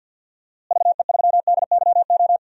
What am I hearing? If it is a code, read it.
Morse code, heard as a keyed tone, about 49 wpm, 700 Hz. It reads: VE3DYO
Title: DXCC-100 Call Signs - 6 Characters - Repeated 3x - 50wpm